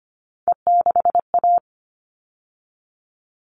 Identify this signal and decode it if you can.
Morse code, heard as a keyed tone, about 25 wpm, 700 Hz. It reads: E6A
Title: All Cut Numbers - 3-Digit 25wpm